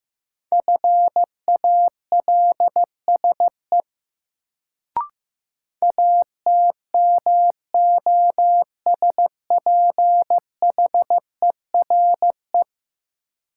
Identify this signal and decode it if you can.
Morse code, heard as a keyed tone, about 15 wpm, 700 Hz. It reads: FALSE ATMOSPHERE
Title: Sentences from Top 1600 Words - New Word Drill Alt - 15wpm